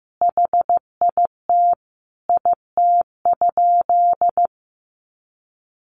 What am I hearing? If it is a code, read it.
Morse code, heard as a keyed tone, about 15 wpm, 700 Hz. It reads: HIT IT?
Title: Sentences from Top 1200 Words 15wpm